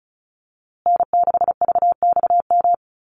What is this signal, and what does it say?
Morse code, heard as a keyed tone, about 34 wpm, 700 Hz. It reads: N6VXM
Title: USA Call Signs Repeated 3x 35wpm